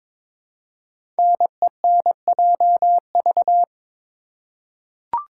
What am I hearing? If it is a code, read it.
Morse code, heard as a keyed tone, about 22 wpm, 700 Hz. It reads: NENJV
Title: Code Groups - 5 Letters - 22wpm